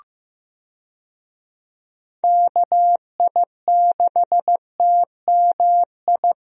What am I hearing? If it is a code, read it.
Morse code, heard as a keyed tone, about 16 wpm, 700 Hz. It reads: KI6TMI
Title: USA Call Signs - 6 Characters - Repeated 3x - 15wpm